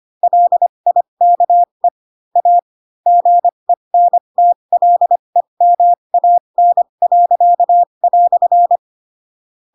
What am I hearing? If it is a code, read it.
Morse code, heard as a keyed tone, about 25 wpm, spaced as for 22 wpm, 700 Hz. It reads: LIKE A GENTLEMAN."